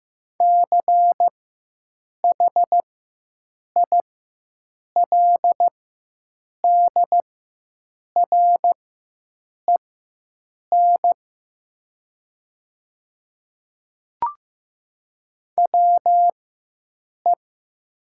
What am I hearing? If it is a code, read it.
Morse code, heard as a keyed tone, about 15 wpm, spaced as for 7 wpm, 700 Hz. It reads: CHILDREN WE
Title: 4x Character Spacing - Top 300 Words - 15wpm